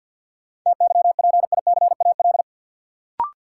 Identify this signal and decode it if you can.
Morse code, heard as a keyed tone, about 52 wpm, 700 Hz. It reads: TYPICAL
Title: Sentences from Top 1100 Words - New Word Drill Alt - 50wpm